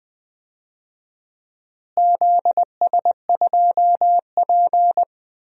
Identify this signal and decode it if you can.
Morse code, heard as a keyed tone, about 20 wpm, 700 Hz. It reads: ZS2P
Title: DXCC-100 Call Signs - 4 Characters - Repeated 2x - 20wpm